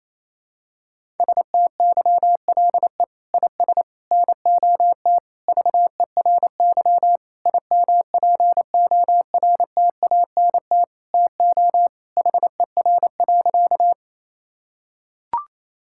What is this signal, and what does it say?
Morse code, heard as a keyed tone, about 28 wpm, 700 Hz. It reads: STYLE IS NOT VERY IMPORTANT TO HER.
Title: Sentences from Top 1000 Words - Review Part 5 - 28wpm